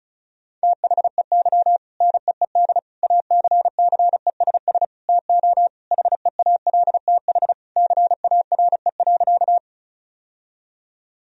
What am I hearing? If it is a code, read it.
Morse code, heard as a keyed tone, about 36 wpm, 700 Hz. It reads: THEY NEED ACCESS TO HEALTH CARE.
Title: Sentences from Top 1000 Words - Review Part 4 - 35wpm